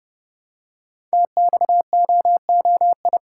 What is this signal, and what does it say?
Morse code, heard as a keyed tone, about 30 wpm, 700 Hz. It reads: TXOOI
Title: Code Groups - 5 Letters - 30wpm